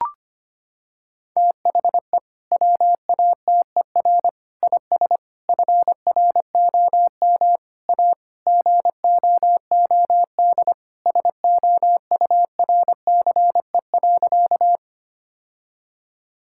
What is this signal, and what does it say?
Morse code, heard as a keyed tone, about 25 wpm, 700 Hz. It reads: THE WATER IS FROM A GOOD SOURCE.
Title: Sentences from Top 1000 Words - Review Part 5 - 25wpm